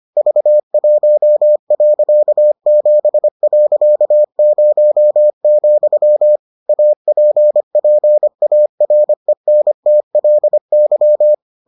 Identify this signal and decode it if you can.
Morse code, heard as a keyed tone, about 25 wpm, 600 Hz. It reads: V1.7.0, APPARENTLY